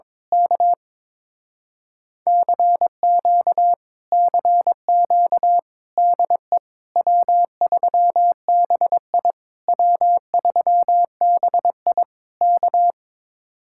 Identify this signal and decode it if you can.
Morse code, heard as a keyed tone, about 22 wpm, 700 Hz. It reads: K CQ CQ DE W3BI W3BI K
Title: Speed-Racing Common QSO Phrases Encore - 22wpm